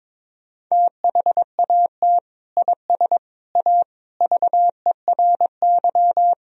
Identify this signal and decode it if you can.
Morse code, heard as a keyed tone, about 22 wpm, 700 Hz. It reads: THAT IS A VERY
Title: Sentences from Top 1000 Words - Review Part 4 - 22wpm